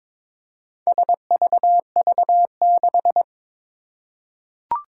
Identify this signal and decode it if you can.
Morse code, heard as a keyed tone, about 22 wpm, 700 Hz. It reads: SVV6
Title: Four Challenging Letter-Number Combinations 22wpm